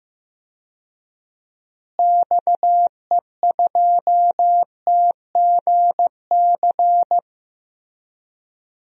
Morse code, heard as a keyed tone, about 15 wpm, 700 Hz. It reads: XE2TGC
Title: DXCC-100 Call Signs - 6 Characters - Repeated 3x - 15wpm